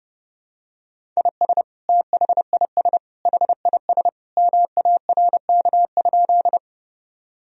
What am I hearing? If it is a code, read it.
Morse code, heard as a keyed tone, about 30 wpm, 700 Hz. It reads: IS THIS HIS MARK?
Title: Sentences from Top 1000 Words - Review Part 1 - 30wpm